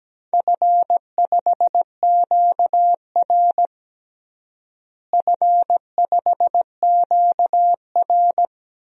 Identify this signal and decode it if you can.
Morse code, heard as a keyed tone, about 17 wpm, 700 Hz. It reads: F5QR F5QR
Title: DXCC-100 Call Signs - 4 Characters - Repeated 2x - 17wpm